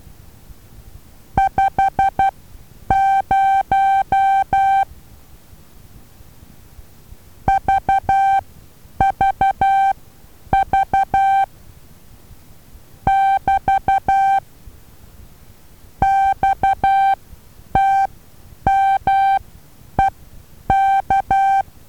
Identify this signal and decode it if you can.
Morse code, heard as a keyed tone, about 12 wpm, spaced as for 9 wpm, 780 Hz. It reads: 50 VVV <BT> XTMEK
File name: Taviro HA6VA B